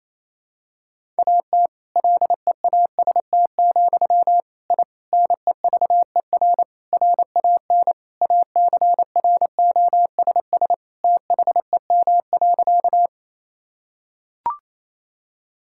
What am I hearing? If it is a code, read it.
Morse code, heard as a keyed tone, about 28 wpm, 700 Hz. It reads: AT LEAST, I NEVER RAN ACROSS THEM.
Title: Sentences from Top 1000 Words - Review Part 5 - 28wpm